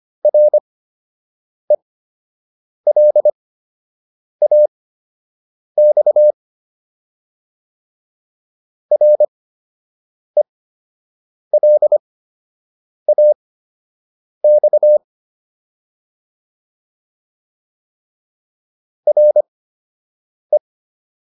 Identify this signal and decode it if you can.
Morse code, heard as a keyed tone, about 25 wpm, spaced as for 7 wpm, 600 Hz. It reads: RELAX RELAX RE